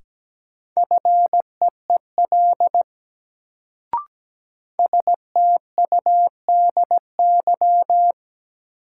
Morse code, heard as a keyed tone, about 17 wpm, 700 Hz. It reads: FEEL STUDY
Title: Sets of 5 Words - New Word Drill - 17wpm